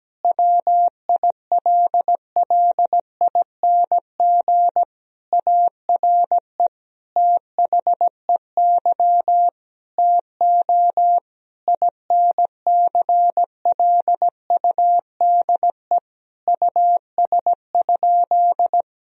Morse code, heard as a keyed tone, about 17 wpm, 700 Hz. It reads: WILLING ARE THEY TO INCLUDE US?
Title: Sentences from Top 1000 Words - Review Part 3 - 17wpm